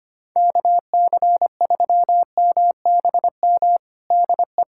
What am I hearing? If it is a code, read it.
Morse code, heard as a keyed tone, about 25 wpm, 700 Hz. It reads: KC3MBM DE
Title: USA Callsign DE Callsign 25wpm